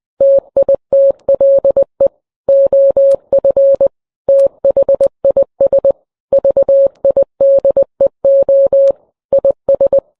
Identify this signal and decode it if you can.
Morse code, heard as a keyed tone, about 20 wpm, 550 Hz. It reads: TITLE OF THIS VIDEO IS